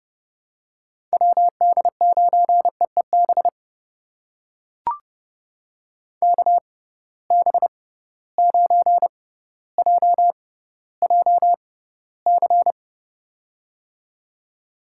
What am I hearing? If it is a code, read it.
Morse code, heard as a keyed tone, about 30 wpm, spaced as for 15 wpm, 700 Hz. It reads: WD9EEB KB9JJC
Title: USA Call Signs - 6 Characters - 6x Character Spacing - 30wpm